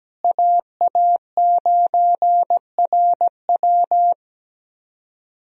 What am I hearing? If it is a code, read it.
Morse code, heard as a keyed tone, about 17 wpm, 700 Hz. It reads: AA9RW
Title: USA Call Signs - 5 Characters - Repeated 3x - 17wpm